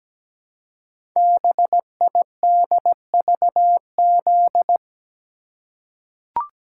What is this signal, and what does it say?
Morse code, heard as a keyed tone, about 17 wpm, 700 Hz. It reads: BIDVZ
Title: Code Groups - 5 Letter:Numbers - 17wpm